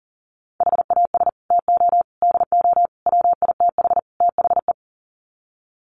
Morse code, heard as a keyed tone, about 40 wpm, 700 Hz. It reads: HAS TO DO WITH THE
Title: Sets of 5 Words 40wpm